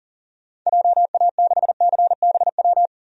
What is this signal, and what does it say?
Morse code, heard as a keyed tone, about 40 wpm, 700 Hz. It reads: JA6CBW
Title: DXCC-100 Call Signs - 6 Characters - 40wpm